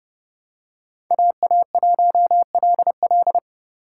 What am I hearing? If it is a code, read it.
Morse code, heard as a keyed tone, about 30 wpm, 700 Hz. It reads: AA1LL
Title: USA Call Signs - 5 Characters - 6x Character Spacing - 30wpm